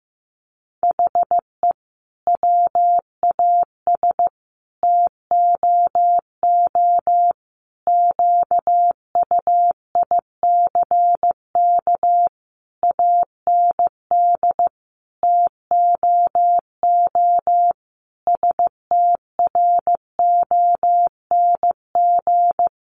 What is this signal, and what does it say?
Morse code, heard as a keyed tone, about 15 wpm, 700 Hz. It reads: HE WAS TOO QUICK AND TOO STRONG
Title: Sentences from Top 600 Words 15wpm